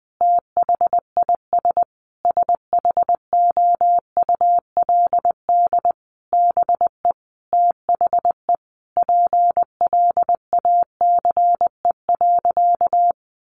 Morse code, heard as a keyed tone, about 20 wpm, 700 Hz. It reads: THIS SHOULD BE THE PLACE.
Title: Sentences from Top 1300 Words 20wpm